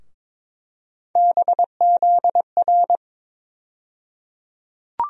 Morse code, heard as a keyed tone, about 22 wpm, 700 Hz. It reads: BZR
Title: USA Call Sign Postfixes - 3 Letters 22wpm